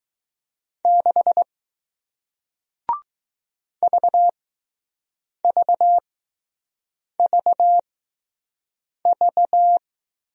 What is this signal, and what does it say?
Morse code, heard as a keyed tone, about 20 wpm, spaced as for 6 wpm, 700 Hz. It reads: 6 VVVV